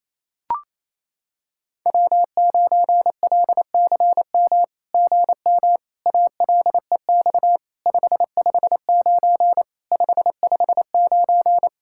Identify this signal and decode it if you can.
Morse code, heard as a keyed tone, about 28 wpm, 700 Hz. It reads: W9LCM GM ALEX 559 559
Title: SOTA QSO Practice 28wpm